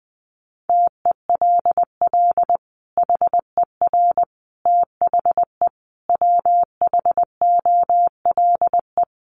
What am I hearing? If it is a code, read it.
Morse code, heard as a keyed tone, about 20 wpm, 700 Hz. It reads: TELL HER THE WHOLE
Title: Sentences from Top 800 Words 20wpm